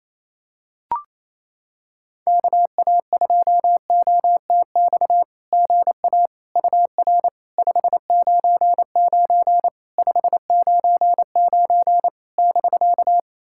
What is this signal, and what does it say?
Morse code, heard as a keyed tone, about 28 wpm, 700 Hz. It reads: KA2OTX GA UR 599 599 <BK>
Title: SOTA QSO Practice 28wpm